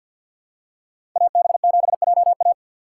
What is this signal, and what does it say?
Morse code, heard as a keyed tone, about 53 wpm, 700 Hz. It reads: AB7JA
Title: USA Call Signs - 5 Characters - 6x Character Spacing - 50wpm